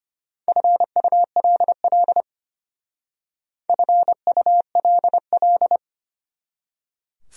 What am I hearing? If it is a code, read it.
Morse code, heard as a keyed tone, about 33 wpm, 700 Hz. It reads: FULL FULL